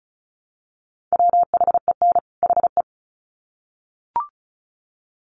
Morse code, heard as a keyed tone, about 34 wpm, 700 Hz. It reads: WHEN HE
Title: Sets of 2 Words 35wpm